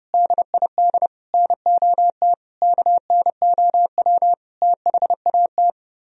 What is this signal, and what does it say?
Morse code, heard as a keyed tone, about 35 wpm, 700 Hz. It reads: DID NOT KNOW THAT
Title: Speed-Racing - Sets of 5 Words 30wpm